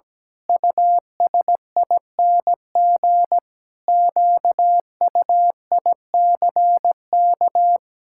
Morse code, heard as a keyed tone, about 17 wpm, 700 Hz. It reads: USING QUICK
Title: Pangram Challenge v2 - 17wpm